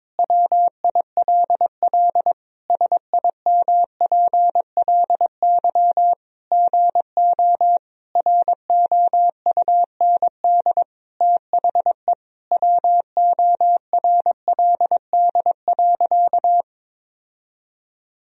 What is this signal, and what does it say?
Morse code, heard as a keyed tone, about 21 wpm, 700 Hz. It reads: WILL SIMPLY GO ROUND THE WORLD.
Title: Sentences from Top 1000 Words - Review Part 4 - 22wpm